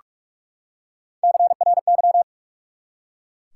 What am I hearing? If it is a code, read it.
Morse code, heard as a keyed tone, about 41 wpm, 700 Hz. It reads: CRY
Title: Three Letter Words - 45wpm